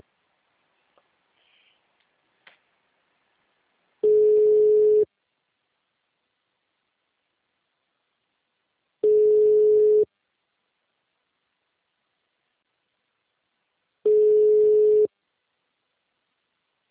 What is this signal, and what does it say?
Ringback tone (Europe, 425 Hz).
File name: out- -2014- - - .90246